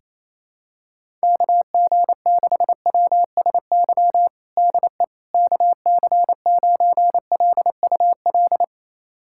Morse code, heard as a keyed tone, about 28 wpm, 700 Hz. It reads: KG6WSY DE KC9LUL
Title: USA Callsign DE Callsign - 6 Characters - 28wpm